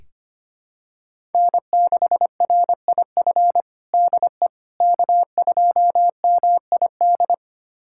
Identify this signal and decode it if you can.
Morse code, heard as a keyed tone, about 25 wpm, 700 Hz. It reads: N6RIF DE K2MID
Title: USA Callsign DE Callsign - 5 Characters - 25wpm